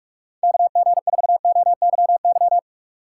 Morse code, heard as a keyed tone, about 44 wpm, 700 Hz. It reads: KG4OYY